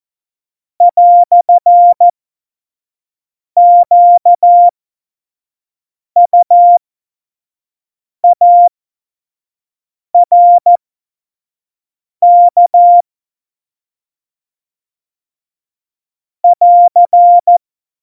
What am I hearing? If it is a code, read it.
Morse code, heard as a keyed tone, about 14 wpm, spaced as for 5 wpm, 700 Hz. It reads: "QUARK <AR>
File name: Morsekurs / Witze / Witz 10 April 2018, 25 F70